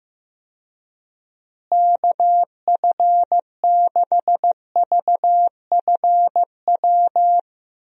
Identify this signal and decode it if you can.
Morse code, heard as a keyed tone, about 15 wpm, 700 Hz. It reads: KF6VFW
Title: USA Call Signs - 6 Characters - Repeated 3x - 15wpm